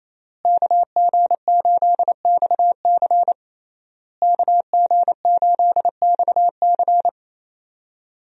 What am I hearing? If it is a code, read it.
Morse code, heard as a keyed tone, about 28 wpm, 700 Hz. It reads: KG8XC KG8XC
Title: USA Call Signs - 5 Characters - Repeated 3x - 28wpm